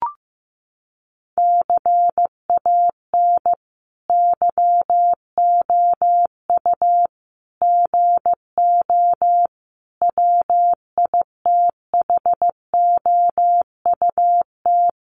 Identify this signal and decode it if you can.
Morse code, heard as a keyed tone, about 15 wpm, 700 Hz. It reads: CAN YOU GO WITHOUT
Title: Sentences from Top 1500 Words 15wpm